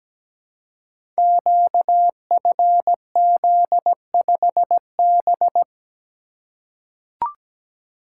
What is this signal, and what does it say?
Morse code, heard as a keyed tone, about 17 wpm, 700 Hz. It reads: QFZ5B
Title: Code Groups - 5 Letter:Numbers - 17wpm